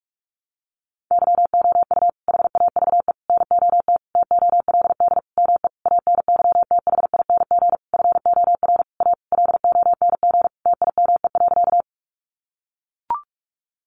Morse code, heard as a keyed tone, about 44 wpm, 700 Hz. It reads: YOU HAVE NOT TOLD ME ANYTHING FOR A LONG TIME.
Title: Sentences from Top 1000 Words 45wpm